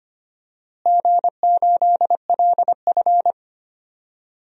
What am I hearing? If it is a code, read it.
Morse code, heard as a keyed tone, about 25 wpm, 700 Hz. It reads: G8LF
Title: DXCC-100 Call Signs - 4 Characters - Repeated 3x - 25wpm